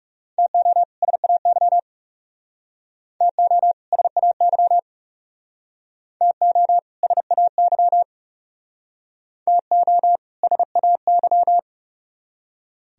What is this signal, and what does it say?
Morse code, heard as a keyed tone, about 40 wpm, 700 Hz. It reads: TO SAY TO SAY TO SAY TO SAY